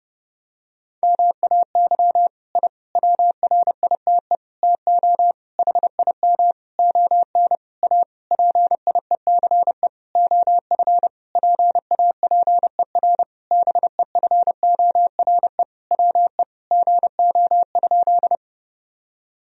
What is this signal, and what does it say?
Morse code, heard as a keyed tone, about 30 wpm, 700 Hz. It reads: MAY I WRITE TO HIM ON A PIECE OF PAPER BEFORE WE GO?
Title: Sentences from Top 1000 Words - Review Part 2 - 30wpm